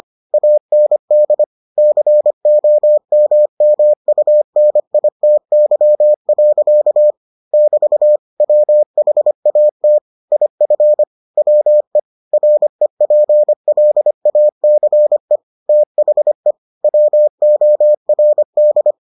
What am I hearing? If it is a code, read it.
Morse code, heard as a keyed tone, about 25 wpm, 600 Hz. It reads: AND COMMUNITY. <BT> WHAT IF WE REPLACE THE WORD